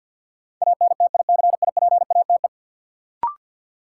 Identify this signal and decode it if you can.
Morse code, heard as a keyed tone, about 52 wpm, 700 Hz. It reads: ANTICIPATE
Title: Sentences from Top 1500 Words - New Word Drill - 50wpm